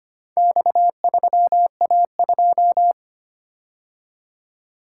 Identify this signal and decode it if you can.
Morse code, heard as a keyed tone, about 25 wpm, 700 Hz. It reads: X3A2
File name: practice / 25wpm / Four Letter-Number Combinations - 25wpm